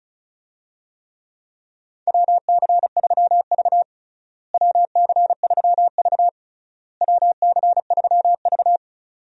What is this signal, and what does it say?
Morse code, heard as a keyed tone, about 36 wpm, 700 Hz. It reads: WC3V WC3V WC3V
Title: USA Call Signs - 4 Characters - Repeated 3x - 35wpm